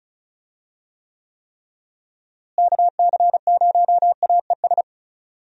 Morse code, heard as a keyed tone, about 34 wpm, 700 Hz. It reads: KC0AES
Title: USA Call Signs - 6 Characters - Repeated 3x - 35wpm